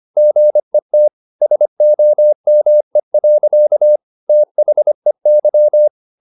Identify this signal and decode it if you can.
Morse code, heard as a keyed tone, about 25 wpm, 600 Hz. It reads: GET SOME. THEY
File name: the old man and the sea part 1 25wpm